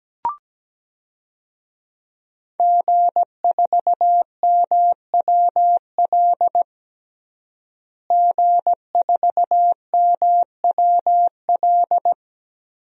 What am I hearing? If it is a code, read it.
Morse code, heard as a keyed tone, about 17 wpm, 700 Hz. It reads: G4MWL G4MWL
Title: DXCC-100 Call Signs - 5 Characters - Repeated 2x - 17wpm